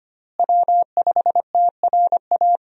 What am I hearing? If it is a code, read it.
Morse code, heard as a keyed tone, about 25 wpm, 700 Hz. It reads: W5TRA
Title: USA Call Signs - 5 Characters - Repeated 2x - 25wpm